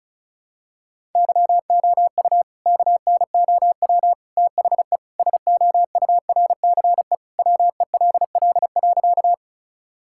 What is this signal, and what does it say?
Morse code, heard as a keyed tone, about 34 wpm, 700 Hz. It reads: YOU KNOW THE SOURCE WELL.
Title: Sentences from Top 1000 Words - Review Part 5 - 35wpm